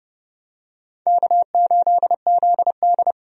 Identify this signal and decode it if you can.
Morse code, heard as a keyed tone, about 30 wpm, 700 Hz. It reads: K8ZD
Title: USA Call Signs - 4 Characters - 30wpm